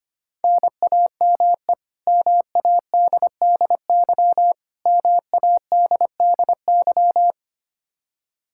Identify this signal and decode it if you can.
Morse code, heard as a keyed tone, about 25 wpm, 700 Hz. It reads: NAME MADDY MADDY
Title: Practice QSO - Exchange 1 - Beginning - 25wpm